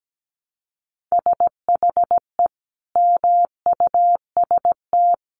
Morse code, heard as a keyed tone, about 17 wpm, 700 Hz. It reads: SHE MUST
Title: Sentences from Top 300 Words 17wpm